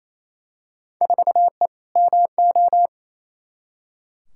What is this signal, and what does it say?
Morse code, heard as a keyed tone, about 28 wpm, 700 Hz. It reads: 4E MO
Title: Field Day Exchanges Encore - 28wpm